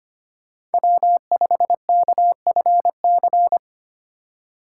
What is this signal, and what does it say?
Morse code, heard as a keyed tone, about 31 wpm, 700 Hz. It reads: W5KFC
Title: Speed Racing - USA 1x3 Call Signs 25wpm